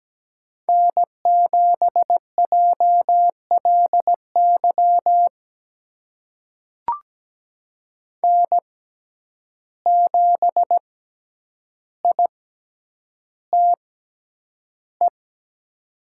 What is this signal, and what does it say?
Morse code, heard as a keyed tone, about 17 wpm, spaced as for 9 wpm, 700 Hz. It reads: N7JLY N7ITE
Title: USA Call Signs - 5 Characters - 6x Character Spacing - 17wpm